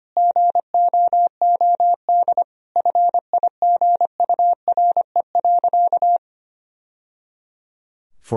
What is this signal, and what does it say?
Morse code, heard as a keyed tone, about 25 wpm, 700 Hz. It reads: GOOD FIGURE.
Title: Sentences from Top 1000 Words - Review Part 2 - 25wpm